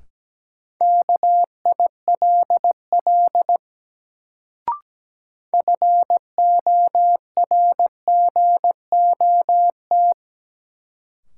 Morse code, heard as a keyed tone, about 17 wpm, 700 Hz. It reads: KILL FORGOT
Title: Sentences from Top 1400 Words - New Word Drill Alt - 17wpm